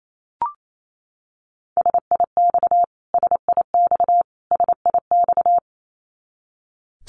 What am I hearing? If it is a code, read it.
Morse code, heard as a keyed tone, about 28 wpm, 700 Hz. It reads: SIX SIX SIX
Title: Top 500 Words Repeated 28wpm